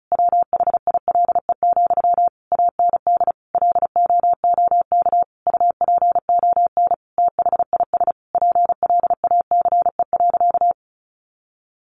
Morse code, heard as a keyed tone, about 34 wpm, 700 Hz. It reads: WHILE, AND LOOK UPON THIS PLACE.
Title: Sentences from Top 1100 Words 35wpm